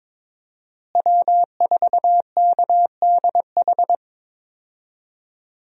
Morse code, heard as a keyed tone, about 21 wpm, 700 Hz. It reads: W4KDH
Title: USA Call Signs - 5 Characters - Repeated 2x - 22wpm